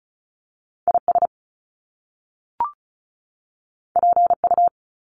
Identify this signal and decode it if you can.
Morse code, heard as a keyed tone, about 36 wpm, 700 Hz. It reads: IS PU
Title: Top Two Letter Combinations in Words 35wpm